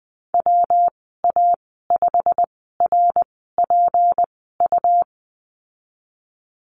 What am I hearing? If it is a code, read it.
Morse code, heard as a keyed tone, about 20 wpm, spaced as for 15 wpm, 700 Hz. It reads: WA5RPU
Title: 154 Lesson 25g - Letter p - Call Signs 20wpm